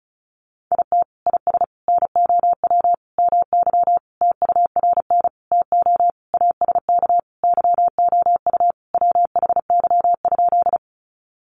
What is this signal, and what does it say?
Morse code, heard as a keyed tone, about 34 wpm, 700 Hz. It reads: IT IS NOW MY TURN TO ASK YOU WHY?
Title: Sentences from Top 200 Words 35wpm